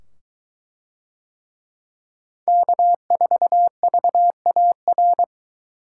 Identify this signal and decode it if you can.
Morse code, heard as a keyed tone, about 21 wpm, 700 Hz. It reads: K4VAR